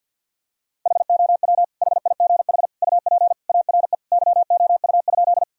Morse code, heard as a keyed tone, about 52 wpm, 700 Hz. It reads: HOW HIGH UP ARE YOU?
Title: Sentences from Top 1000 Words - Review Part 1 - 50wpm